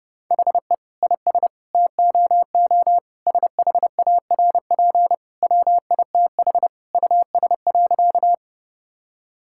Morse code, heard as a keyed tone, about 30 wpm, 700 Hz. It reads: HE IS TOO SHARP WITH US.
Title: Sentences from Top 1600 Words 30wpm